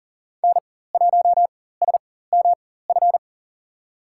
Morse code, heard as a keyed tone, about 40 wpm, spaced as for 19 wpm, 690 Hz. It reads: N1SMF
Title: USA Call Signs - 4x Character Spacing - 40wpm